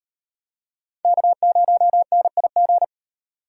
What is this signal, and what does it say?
Morse code, heard as a keyed tone, about 36 wpm, 700 Hz. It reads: K0NIG